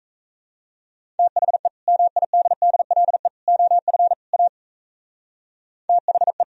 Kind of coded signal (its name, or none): Morse code